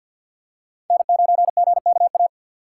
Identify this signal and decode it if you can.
Morse code, heard as a keyed tone, about 47 wpm, 700 Hz. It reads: N9GKA
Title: USA Call Signs - 4x Character Spacing - 50wpm